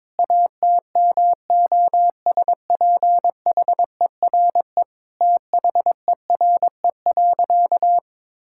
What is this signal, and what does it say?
Morse code, heard as a keyed tone, about 22 wpm, 700 Hz. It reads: ATMOSPHERE THERE.
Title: Sentences from Top 1600 Words 22wpm